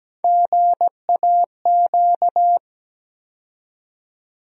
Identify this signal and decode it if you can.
Morse code, heard as a keyed tone, about 17 wpm, 700 Hz. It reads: GAQ
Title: USA Call Sign Postfixes - 3 Letters 17wpm